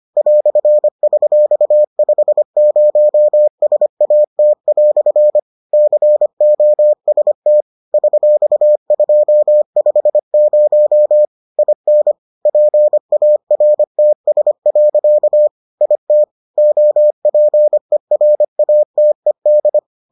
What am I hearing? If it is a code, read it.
Morse code, heard as a keyed tone, about 25 wpm, 600 Hz. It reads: "$50SAT" COST $250 IN PARTS. IT OPERATED